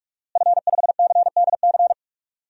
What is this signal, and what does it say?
Morse code, heard as a keyed tone, about 46 wpm, 700 Hz. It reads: F5CDC
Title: DXCC-100 Call Signs - 5 Characters - Repeated 2x - 45wpm